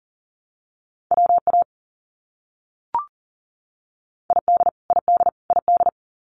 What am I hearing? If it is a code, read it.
Morse code, heard as a keyed tone, about 38 wpm, 700 Hz. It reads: WA ID ID ID
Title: US States Repeated 3x 40wpm